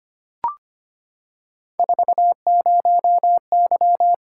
Morse code, heard as a keyed tone, about 25 wpm, 700 Hz. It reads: 40Y
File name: Three Letter-Number Combinations - 25wpm